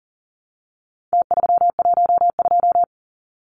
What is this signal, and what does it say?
Morse code, heard as a keyed tone, about 40 wpm, 700 Hz. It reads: T312
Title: Numbers - Contest Style - 4-Digit 40wpm